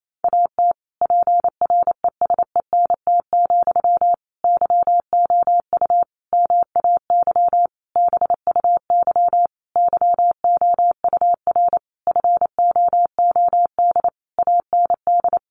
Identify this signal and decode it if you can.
Morse code, heard as a keyed tone, about 28 wpm, 700 Hz. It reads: AT PRESENT, YOU MAY BUY YOUR FOOD AND
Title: Sentences from Top 800 Words 28wpm